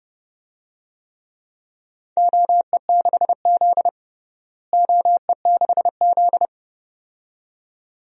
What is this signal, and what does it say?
Morse code, heard as a keyed tone, about 30 wpm, 700 Hz. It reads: OE6Z OE6Z
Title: DXCC-100 Call Signs - 4 Characters - Repeated 2x - 30wpm